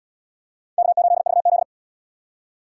Morse code, heard as a keyed tone, about 78 wpm, 700 Hz. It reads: K8VZ